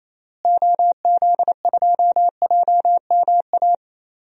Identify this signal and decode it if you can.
Morse code, heard as a keyed tone, about 28 wpm, 700 Hz. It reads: OZ2JMA
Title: DXCC-100 Call Signs - 6 Characters - Repeated 3x - 28wpm